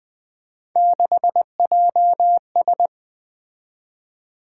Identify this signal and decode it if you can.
Morse code, heard as a keyed tone, about 20 wpm, 700 Hz. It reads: 6JS
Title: Three Letter-Number Combinations - 20wpm